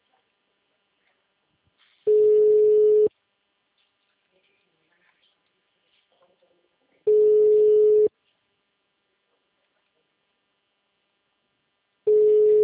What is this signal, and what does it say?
Ringback tone (Europe, 425 Hz).
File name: out- -2032- - - .93271